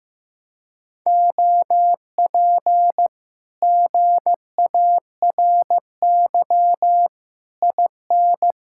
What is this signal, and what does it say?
Morse code, heard as a keyed tone, about 15 wpm, 700 Hz. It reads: OP GARY IN
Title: Speed-Racing Common QSO Phrases Encore - 15wpm